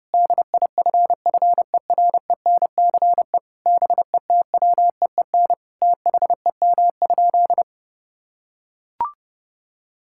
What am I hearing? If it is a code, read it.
Morse code, heard as a keyed tone, about 30 wpm, 700 Hz. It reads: DIFFERENCE BETWEEN THEM?
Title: Sentences from Top 1000 Words - Review Part 5 - 30wpm